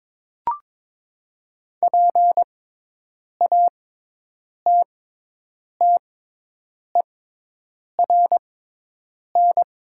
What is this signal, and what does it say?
Morse code, heard as a keyed tone, about 23 wpm, spaced as for 8 wpm, 700 Hz. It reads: PATTERN
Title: 6x Character Spacing - Top 500 Words - 22wpm